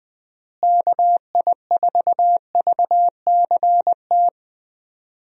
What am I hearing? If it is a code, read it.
Morse code, heard as a keyed tone, about 20 wpm, 700 Hz. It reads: KI4VCT
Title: USA Call Signs - 6 Characters - Repeated 3x - 20wpm